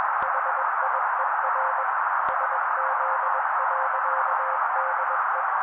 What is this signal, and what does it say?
Morse code, heard as a keyed tone, about 21 wpm, 530 Hz. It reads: HIER SZ.DE